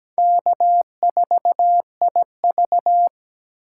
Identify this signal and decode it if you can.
Morse code, heard as a keyed tone, about 20 wpm, 700 Hz. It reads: K4IV
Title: Speed Racing - USA 1x2 Call Signs 17wpm